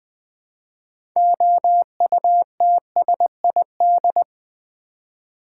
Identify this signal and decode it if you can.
Morse code, heard as a keyed tone, about 20 wpm, 700 Hz. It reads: OUTSID
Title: Build A Word - 3 Letters At A Time - 8-Letter Words - 20wpm